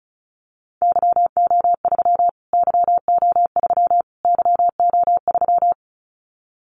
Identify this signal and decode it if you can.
Morse code, heard as a keyed tone, about 34 wpm, 700 Hz. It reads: YO3 YO3 YO3
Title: Callsign Prefixes for Top 25 DXCC - Repeated 3x - 35wpm